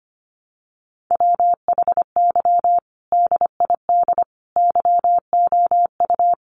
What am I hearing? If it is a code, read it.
Morse code, heard as a keyed tone, about 25 wpm, 700 Hz. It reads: WHY DID YOU
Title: Sentences from Top 1000 Words 25wpm